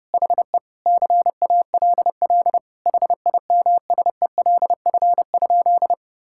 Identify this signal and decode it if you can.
Morse code, heard as a keyed tone, about 30 wpm, 700 Hz. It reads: HE CALL HIMSELF?
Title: Sentences from Top 1600 Words 30wpm